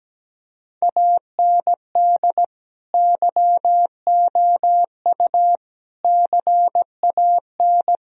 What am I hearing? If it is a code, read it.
Morse code, heard as a keyed tone, about 17 wpm, 700 Hz. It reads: AND YOU CAN
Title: Sentences from Top 1000 Words - Review Part 2 - 17wpm